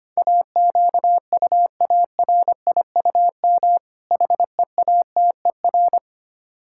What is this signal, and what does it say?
Morse code, heard as a keyed tone, about 25 wpm, 700 Hz. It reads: AQUARIUM HEATER
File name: practice / 25wpm / Difficult Words with Hints - 25wpm